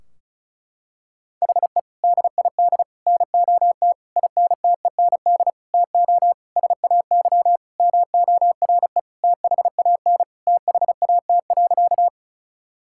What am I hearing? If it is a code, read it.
Morse code, heard as a keyed tone, about 36 wpm, 700 Hz. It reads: HE DID NOT INTEND TO SAY MORE THAN THAT.
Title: Sentences from Top 1600 Words 35wpm